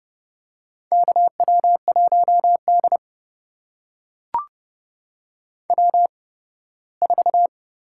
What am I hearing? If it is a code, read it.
Morse code, heard as a keyed tone, about 30 wpm, spaced as for 18 wpm, 700 Hz. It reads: KW1D W4
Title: USA Call Signs - 4 Characters - 8x Character Spacing - 30wpm